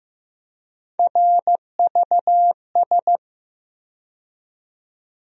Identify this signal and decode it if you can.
Morse code, heard as a keyed tone, about 16 wpm, 700 Hz. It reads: RVS
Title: USA Call Sign Postfixes - 3 Letters 15wpm